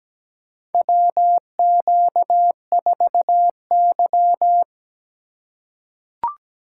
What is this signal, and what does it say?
Morse code, heard as a keyed tone, about 17 wpm, 700 Hz. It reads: WQ4Y
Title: USA Call Signs - 4 Characters - 3x Character Spacing - 17wpm